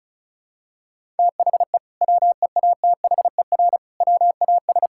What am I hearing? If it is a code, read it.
Morse code, heard as a keyed tone, about 34 wpm, 700 Hz. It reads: THE WEATHER WAS
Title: Sentences from Top 1000 Words - Review Part 5 - 35wpm